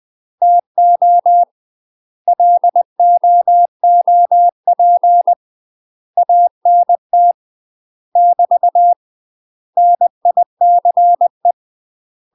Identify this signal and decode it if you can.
Morse code, heard as a keyed tone, about 20 wpm, 700 Hz. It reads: TO LOOP ANT <BT> NICE
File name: UK MASTER QSOs TRAINING / 20WPM / UK MASTER QSOs 0026